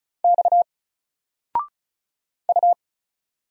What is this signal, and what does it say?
Morse code, heard as a keyed tone, about 36 wpm, 700 Hz. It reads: X U
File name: Single-Letter-Number-35wpm